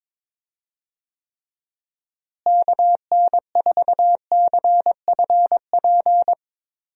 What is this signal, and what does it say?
Morse code, heard as a keyed tone, about 22 wpm, 700 Hz. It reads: KN4CFP
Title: USA Call Signs - 6 Characters - Repeated 3x - 22wpm